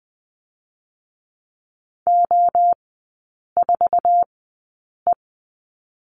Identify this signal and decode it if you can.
Morse code, heard as a keyed tone, about 20 wpm, spaced as for 8 wpm, 700 Hz. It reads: O4E
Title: 053 Lesson 10e - Letter r - Three Characters 20wpm